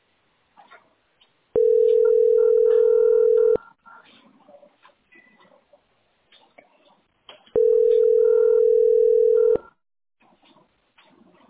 Ringback tone (North America), 440 + 480 Hz.